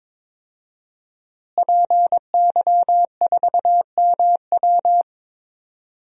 Morse code, heard as a keyed tone, about 22 wpm, 700 Hz. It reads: PY4MW
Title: DXCC-100 Call Signs - 5 Characters - Repeated 3x - 22wpm